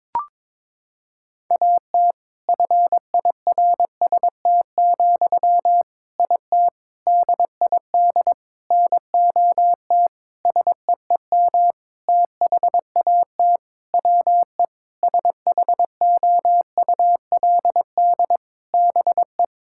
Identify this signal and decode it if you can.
Morse code, heard as a keyed tone, about 23 wpm, 700 Hz. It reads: AT FIRST, IT DID NOT SEEM THAT WE SHOULD BE
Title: Sentences from Top 1000 Words - Review Part 6 - 22wpm